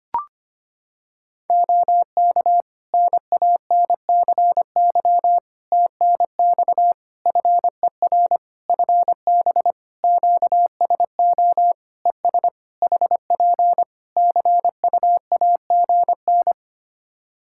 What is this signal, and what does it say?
Morse code, heard as a keyed tone, about 25 wpm, 700 Hz. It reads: OK NANCY TNX FER FB QSO ES HP CUAGN
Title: Practice QSO - Exchange 4 - Ending - 25wpm